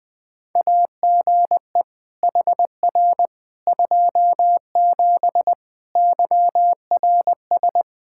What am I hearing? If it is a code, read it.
Morse code, heard as a keyed tone, about 20 wpm, 700 Hz. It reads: AGE HR 27 YRS
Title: Practice QSO - Exchange 3 - More Info - 20wpm